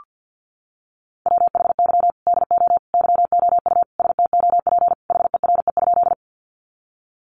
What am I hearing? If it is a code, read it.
Morse code, heard as a keyed tone, about 48 wpm, 700 Hz. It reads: WHY DO YOU STOP HERE?
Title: Sentences from Top 300 Words 50wpm